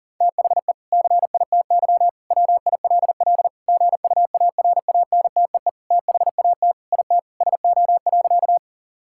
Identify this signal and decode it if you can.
Morse code, heard as a keyed tone, about 40 wpm, 700 Hz. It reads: THE CITY WILL GUARANTEE THAT IT SO.
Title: Sentences from Top 1600 Words 40wpm